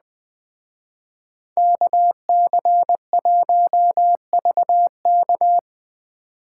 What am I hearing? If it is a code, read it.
Morse code, heard as a keyed tone, about 20 wpm, 700 Hz. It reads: KC1VK
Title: Speed Racing - USA 2x2 Call Signs 17wpm